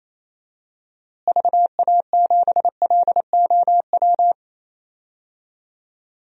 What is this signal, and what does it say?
Morse code, heard as a keyed tone, about 28 wpm, 700 Hz. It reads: VA7LOW
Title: DXCC-100 Call Signs - 6 Characters - Repeated 3x - 28wpm